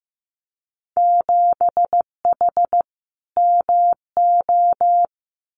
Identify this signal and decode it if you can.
Morse code, heard as a keyed tone, about 15 wpm, 700 Hz. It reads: 7H MO
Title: Winter Field Day Exchanges Encore 15wpm